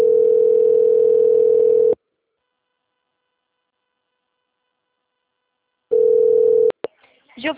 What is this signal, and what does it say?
Ringback tone (North America), 440 + 480 Hz.
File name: out- - - - - .24788